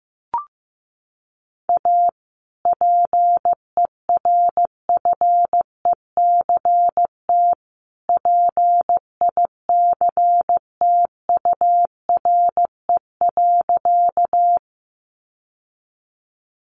Morse code, heard as a keyed tone, about 15 wpm, 700 Hz. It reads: A PERFECT PICTURE.
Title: Sentences from Top 1000 Words 15wpm